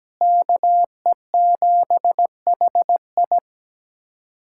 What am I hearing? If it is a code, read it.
Morse code, heard as a keyed tone, about 17 wpm, 700 Hz. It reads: KE7HI
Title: USA Callsign DE Callsign - 5 Characters - 17wpm